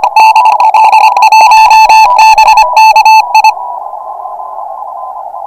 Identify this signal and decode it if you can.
Morse code, heard as a keyed tone, about 25 wpm, 880 Hz. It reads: DL1BKI
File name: attachments / pileup 3